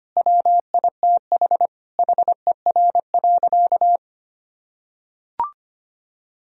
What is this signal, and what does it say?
Morse code, heard as a keyed tone, about 25 wpm, 700 Hz. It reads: WITH HER.
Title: Sentences from Top 1000 Words - Review Part 4 - 25wpm